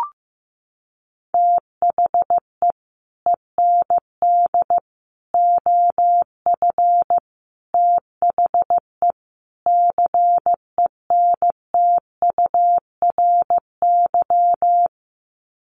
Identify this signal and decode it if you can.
Morse code, heard as a keyed tone, about 15 wpm, 700 Hz. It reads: THE END OF THE CENTURY
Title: Sets of 5 Words Encore2 15wpm